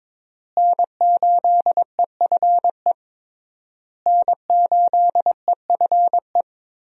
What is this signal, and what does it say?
Morse code, heard as a keyed tone, about 22 wpm, 700 Hz. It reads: N8EFE N8EFE
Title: USA Call Signs - 5 Characters - Repeated 3x - 22wpm